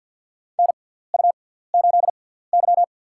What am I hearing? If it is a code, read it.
Morse code, heard as a keyed tone, about 50 wpm, spaced as for 17 wpm, 700 Hz. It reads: NU8Y
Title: USA Call Signs - 4 Characters - 6x Character Spacing - 50wpm